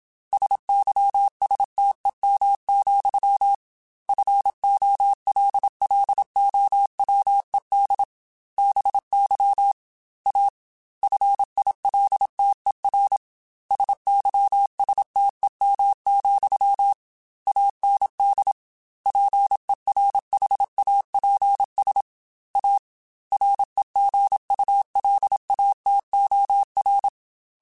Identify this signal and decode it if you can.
Morse code, heard as a keyed tone, about 27 wpm, 800 Hz. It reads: SYSTEM, FOLLOWED BY A FILTER SYSTEM, AND PERHAPS A REGULATOR
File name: Technical Subjects / Technical Subjects-mp3s / Sw Pwr Sup @25